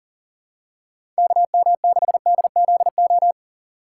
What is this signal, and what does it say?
Morse code, heard as a keyed tone, about 40 wpm, 700 Hz. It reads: KM6DZO
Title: USA Call Signs - 6 Characters - 3x Character Spacing - 40wpm